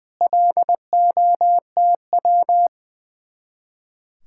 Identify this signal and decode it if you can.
Morse code, heard as a keyed tone, about 21 wpm, 700 Hz. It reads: LOTW
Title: Speed-Racing Common QSO Phrases - 20wpm